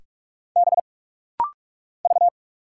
Morse code, heard as a keyed tone, about 44 wpm, 700 Hz. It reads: B V
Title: Single Letter-Number Alt - Rapid-Fire 45wpm